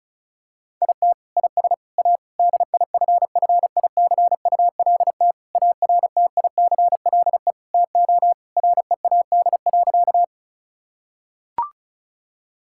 Morse code, heard as a keyed tone, about 36 wpm, 700 Hz. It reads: IT IS A DIFFICULT ARTICLE TO READ.
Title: Sentences from Top 1000 Words - Review Part 5 - 35wpm